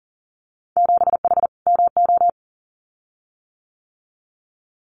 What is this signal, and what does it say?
Morse code, heard as a keyed tone, about 41 wpm, 700 Hz. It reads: 7H MO
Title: Winter Field Day Exchanges Encore 40wpm